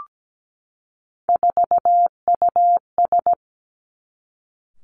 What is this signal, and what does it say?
Morse code, heard as a keyed tone, about 17 wpm, 700 Hz. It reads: 4US
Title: Three Challenging Letter-Number Combinations 17wpm